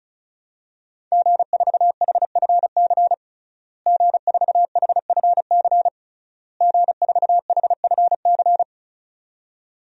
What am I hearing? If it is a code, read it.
Morse code, heard as a keyed tone, about 36 wpm, 700 Hz. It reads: G4HFC G4HFC G4HFC
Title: DXCC-100 Call Signs - 5 Characters - Repeated 3x - 35wpm